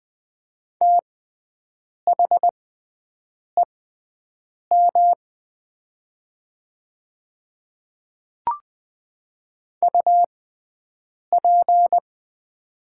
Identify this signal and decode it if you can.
Morse code, heard as a keyed tone, about 20 wpm, spaced as for 7 wpm, 700 Hz. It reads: THEM UP
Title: 6x Character Spacing - Top 100 Words - 20wpm